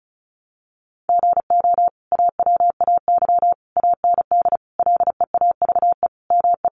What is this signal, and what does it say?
Morse code, heard as a keyed tone, about 34 wpm, 700 Hz. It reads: GO AWAY AND LEAVE ME
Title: Sentences from Top 1300 Words 35wpm